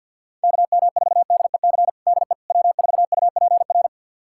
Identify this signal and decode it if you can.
Morse code, heard as a keyed tone, about 49 wpm, 700 Hz. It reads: KM4DEC DE W4UPR
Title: USA Callsign DE Callsign 50wpm